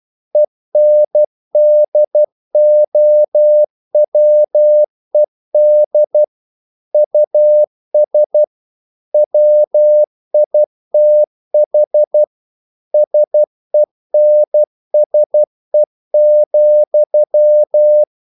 Morse code, heard as a keyed tone, about 12 wpm, 600 Hz. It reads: ENDOWED US WITH SENSE,
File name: CW / Practice / EnglishQuotations0112